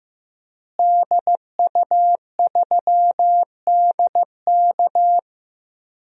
Morse code, heard as a keyed tone, about 15 wpm, 700 Hz. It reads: DU3DK
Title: DXCC-100 Call Signs - 5 Characters - Repeated 3x - 15wpm